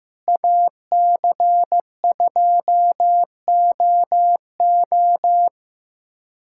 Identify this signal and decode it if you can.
Morse code, heard as a keyed tone, about 16 wpm, 700 Hz. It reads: AC2OO
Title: USA Callsign DE Callsign - 5 Characters - 15wpm